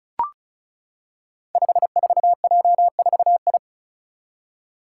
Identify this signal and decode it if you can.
Morse code, heard as a keyed tone, about 36 wpm, 700 Hz. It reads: 54J4I
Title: Five Challenging Letter-Number Combinations 35wpm